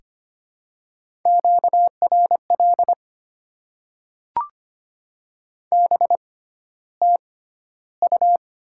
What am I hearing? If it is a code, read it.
Morse code, heard as a keyed tone, about 25 wpm, spaced as for 13 wpm, 700 Hz. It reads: QRL BTU
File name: Top 20 QSO Elements - 6x Character Spacing - 25wpm